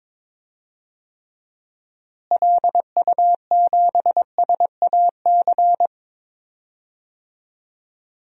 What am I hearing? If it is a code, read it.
Morse code, heard as a keyed tone, about 22 wpm, 700 Hz. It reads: LU7SAC
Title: DXCC-100 Call Signs - 6 Characters - Repeated 3x - 22wpm